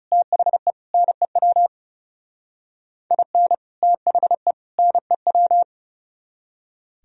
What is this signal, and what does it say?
Morse code, heard as a keyed tone, about 37 wpm, 690 Hz. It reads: THE NEW IN THE NEW
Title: Speed-Racing - Sets of 3 Words 30wpm